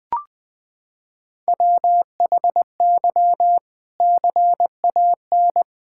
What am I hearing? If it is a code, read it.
Morse code, heard as a keyed tone, about 20 wpm, 700 Hz. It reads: WHY CAN
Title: Sentences from Top 1600 Words 20wpm